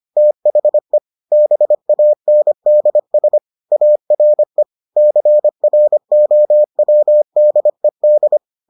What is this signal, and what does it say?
Morse code, heard as a keyed tone, about 25 wpm, 600 Hz. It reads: THE BANDS ARE CROWDED